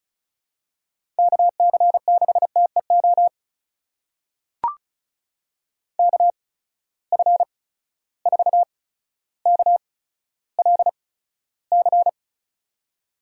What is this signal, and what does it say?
Morse code, heard as a keyed tone, about 36 wpm, spaced as for 15 wpm, 700 Hz. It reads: KC6TEO KF4KLC
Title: USA Call Signs - 6 Characters - 8x Character Spacing - 35wpm